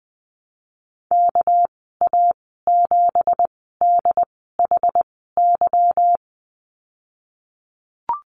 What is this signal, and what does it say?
Morse code, heard as a keyed tone, about 20 wpm, spaced as for 15 wpm, 700 Hz. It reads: KA7DHY
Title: 219 Lesson 35g - Number 0 - Call Signs 20wpm